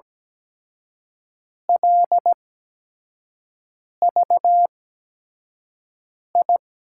Morse code, heard as a keyed tone, about 17 wpm, spaced as for 5 wpm, 700 Hz. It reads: LVI
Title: USA Call Signs - 6 Characters - 8x Character Spacing - 17wpm